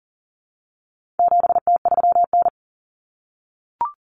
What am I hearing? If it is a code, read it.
Morse code, heard as a keyed tone, about 40 wpm, 700 Hz. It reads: 7T3N
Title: Common Cut Numbers - 4-Digit 40wpm